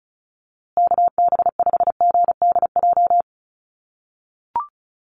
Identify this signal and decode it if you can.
Morse code, heard as a keyed tone, about 34 wpm, 700 Hz. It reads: KB5GDJ
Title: USA Call Signs Repeated 2x 35wpm